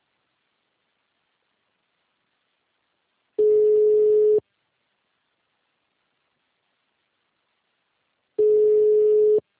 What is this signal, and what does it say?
Ringback tone (Europe, 425 Hz).